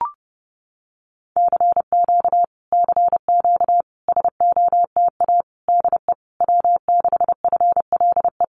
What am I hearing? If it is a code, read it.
Morse code, heard as a keyed tone, about 30 wpm, 700 Hz. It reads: CQ CQ SOTA DE W6FLE
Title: Common QSO Phrases Encore 30wpm